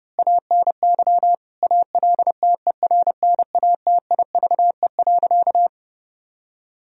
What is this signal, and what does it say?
Morse code, heard as a keyed tone, about 32 wpm, 700 Hz. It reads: ANY ALTERNATIVE.
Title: Sentences from Top 1000 Words - Review Part 4 - 30wpm